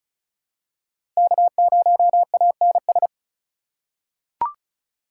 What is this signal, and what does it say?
Morse code, heard as a keyed tone, about 36 wpm, 700 Hz. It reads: K0ANS
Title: USA Call Signs - 5 Characters - 3x Character Spacing - 35wpm